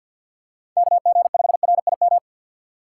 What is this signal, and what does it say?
Morse code, heard as a keyed tone, about 53 wpm, 700 Hz. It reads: KG5RIM
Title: USA Call Signs - 6 Characters - 8x Character Spacing - 50wpm